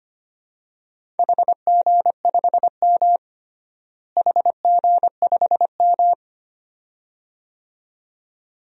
Morse code, heard as a keyed tone, about 25 wpm, 700 Hz. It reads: HG5M HG5M
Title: DXCC-100 Call Signs - 4 Characters - Repeated 2x - 25wpm